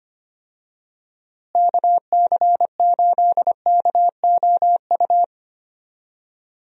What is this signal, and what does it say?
Morse code, heard as a keyed tone, about 25 wpm, 700 Hz. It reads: KC8KOU
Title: USA Call Signs - 6 Characters - Repeated 3x - 25wpm